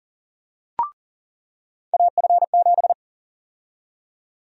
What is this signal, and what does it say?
Morse code, heard as a keyed tone, about 40 wpm, 700 Hz. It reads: AF7
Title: USA Call Sign Prefixes - 40wpm